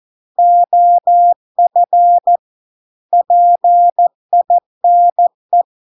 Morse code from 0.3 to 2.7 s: OF